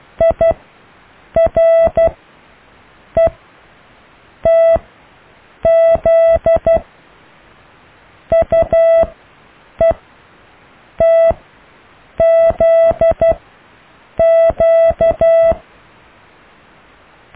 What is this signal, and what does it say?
Morse code, heard as a keyed tone, about 12 wpm, spaced as for 7 wpm, 660 Hz. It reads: IRETZ UETZQ
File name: cw / Alle letters